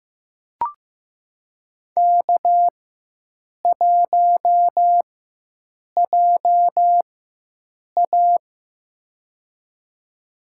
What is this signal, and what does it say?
Morse code, heard as a keyed tone, about 15 wpm, spaced as for 7 wpm, 700 Hz. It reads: K1JA
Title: USA Call Signs - 4 Characters - 4x Character Spacing - 15wpm